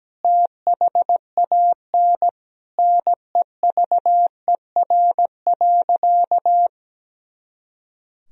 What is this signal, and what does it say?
Morse code, heard as a keyed tone, about 17 wpm, 700 Hz. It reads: THAN NEVER.
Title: Sentences from Top 400 Words and Abbreviations 17wpm